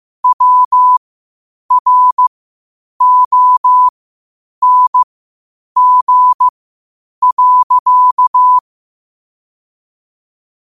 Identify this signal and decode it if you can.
Morse code, heard as a keyed tone, about 15 wpm, spaced as for 8 wpm, 1000 Hz. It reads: WRONG.
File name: qotdfiles / 8wpm